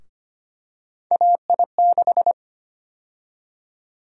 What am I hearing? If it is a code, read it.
Morse code, heard as a keyed tone, about 25 wpm, 700 Hz. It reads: AI6
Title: USA Call Sign Prefixes - Repeated 2x - 25wpm